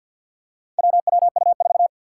Morse code, heard as a keyed tone, about 47 wpm, 700 Hz. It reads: WWU4
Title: Four Challenging Letter-Number Combinations 50wpm